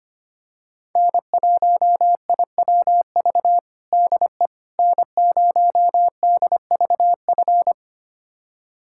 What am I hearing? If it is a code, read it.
Morse code, heard as a keyed tone, about 25 wpm, 700 Hz. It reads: N1IWV DE N0DVF
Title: USA Callsign DE Callsign - 5 Characters - 25wpm